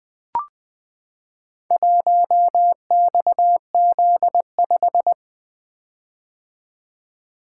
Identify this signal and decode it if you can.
Morse code, heard as a keyed tone, about 20 wpm, 700 Hz. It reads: 1XZ5
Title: Four Letter-Number Combinations - 20wpm